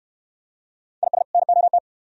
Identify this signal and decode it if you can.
Morse code, heard as a keyed tone, about 68 wpm, 700 Hz. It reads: IS NOT